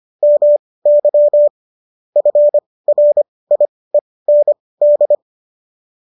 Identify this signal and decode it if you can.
Morse code, heard as a keyed tone, about 25 wpm, spaced as for 18 wpm, 590 Hz. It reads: MY FRIEND
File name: qso218 18